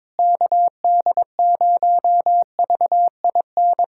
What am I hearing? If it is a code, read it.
Morse code, heard as a keyed tone, about 22 wpm, 700 Hz. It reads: KD0VIN
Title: USA Callsign DE Callsign 22wpm